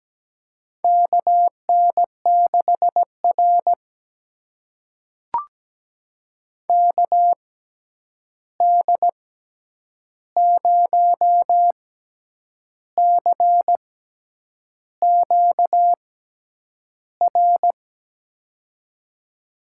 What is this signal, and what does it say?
Morse code, heard as a keyed tone, about 17 wpm, spaced as for 8 wpm, 700 Hz. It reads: KN6R KD0CQR
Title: USA Call Signs - 6x Character Spacing - 17wpm